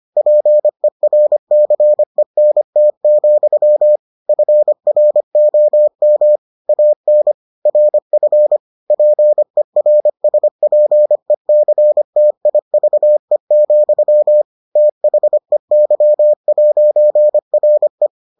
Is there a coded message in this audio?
Yes: Morse code from 0.2 to 18.1 s, heard as a keyed tone, about 25 wpm, 600 Hz. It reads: PERCENT, FROM AN RF PERSPECTIVE, THEY'RE